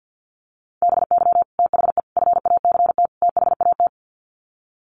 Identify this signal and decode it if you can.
Morse code, heard as a keyed tone, about 50 wpm, 700 Hz. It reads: BY THE FACT THAT
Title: Sets of 4 Words 50wpm